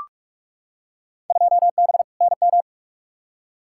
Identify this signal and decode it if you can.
Morse code, heard as a keyed tone, about 44 wpm, 700 Hz. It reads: 2B NM
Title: Field Day Exchanges Encore - 45wpm